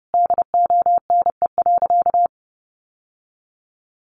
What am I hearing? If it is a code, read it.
Morse code, heard as a keyed tone, about 30 wpm, 700 Hz. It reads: DONE.
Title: Sentences from Top 700 Words 30wpm